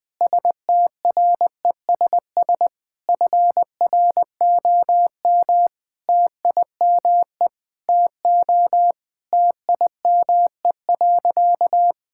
Morse code, heard as a keyed tone, about 21 wpm, 700 Hz. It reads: STRESS FROM TIME TO TIME.
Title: Sentences from Top 1000 Words - Review Part 6 - 20wpm